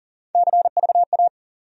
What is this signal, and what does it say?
Morse code, heard as a keyed tone, about 40 wpm, 700 Hz. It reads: CVA
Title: USA Call Sign Postfixes - 3 Letters 40wpm